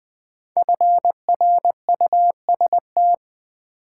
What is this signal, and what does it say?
Morse code, heard as a keyed tone, about 20 wpm, 700 Hz. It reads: FRUST
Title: Build A Word - 5 Letters At A Time - 10-Letter Words - 20wpm